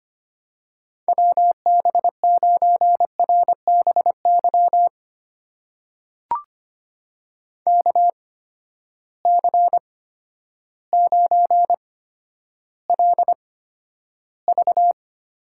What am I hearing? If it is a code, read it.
Morse code, heard as a keyed tone, about 25 wpm, spaced as for 11 wpm, 700 Hz. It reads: WB9RBY KC9LV
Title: USA Call Signs - 6 Characters - 8x Character Spacing - 25wpm